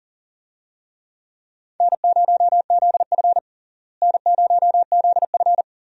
Morse code, heard as a keyed tone, about 40 wpm, 700 Hz. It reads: N0ZF N0ZF
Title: USA Call Signs - 4 Characters - Repeated 3x - 40wpm